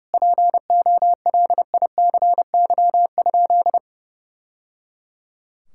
Morse code, heard as a keyed tone, about 30 wpm, 700 Hz. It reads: POLICY?
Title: Sentences from Top 1000 Words - Review Part 5 - 30wpm